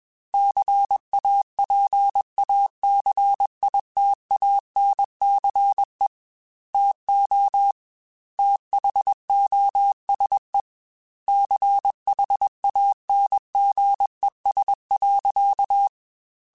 Morse code, heard as a keyed tone, about 21 wpm, 800 Hz. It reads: CAPACITANCE TO THOSE CHANGES.
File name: Technical Subjects / Technical Subjects-mp3s / XC @20